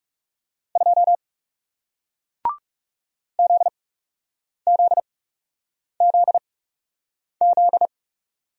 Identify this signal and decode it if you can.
Morse code, heard as a keyed tone, about 39 wpm, spaced as for 8 wpm, 700 Hz. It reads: 2 ZZZZ